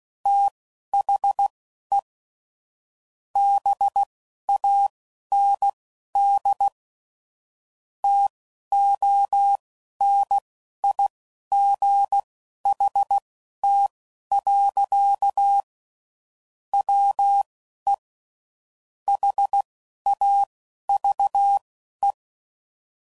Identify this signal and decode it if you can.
Morse code, heard as a keyed tone, about 16 wpm, spaced as for 11 wpm, 800 Hz. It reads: THE BAND TONIGHT. WE HAVE
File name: CW Code Groups @15WPM / QSOs / WD0BGP Spaced